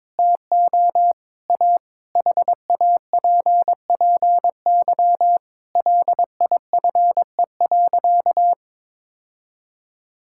Morse code, heard as a keyed tone, about 22 wpm, 700 Hz. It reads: TO A HAPPY LIFE.
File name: Difficult Sentences with Hints - 22wpm